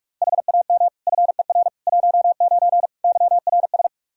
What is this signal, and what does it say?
Morse code, heard as a keyed tone, about 46 wpm, 700 Hz. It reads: HAM FER 19 YRS
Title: K1USN SST QSO Practice 45wpm